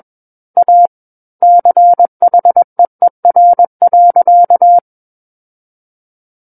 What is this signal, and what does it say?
Morse code, heard as a keyed tone, about 21 wpm, 700 Hz. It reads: A CHEER.
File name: Book One / 12-Chapter 12 - 21 WPM